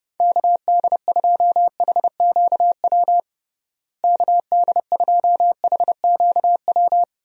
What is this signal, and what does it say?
Morse code, heard as a keyed tone, about 30 wpm, 700 Hz. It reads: KD2HQW KD2HQW
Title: USA Call Signs - 6 Characters - Repeated 2x - 30wpm